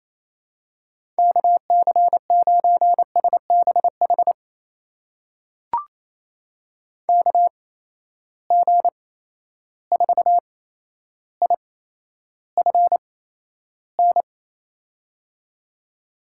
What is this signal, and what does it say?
Morse code, heard as a keyed tone, about 28 wpm, spaced as for 12 wpm, 700 Hz. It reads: KC9SBH KG4IFN
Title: USA Call Signs - 8x Character Spacing - 28wpm